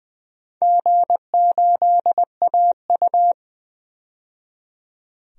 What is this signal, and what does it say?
Morse code, heard as a keyed tone, about 20 wpm, 700 Hz. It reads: G8AU
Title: DXCC-100 Call Signs - 4 Characters - 20wpm